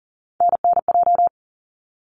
Morse code, heard as a keyed tone, about 40 wpm, 700 Hz. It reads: NNJ
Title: ARRL Sections 40wpm